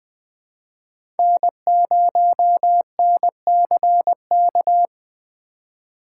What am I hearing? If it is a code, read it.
Morse code, heard as a keyed tone, about 20 wpm, 700 Hz. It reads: N0NCK
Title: USA Call Signs - 8x Character Spacing - 20wpm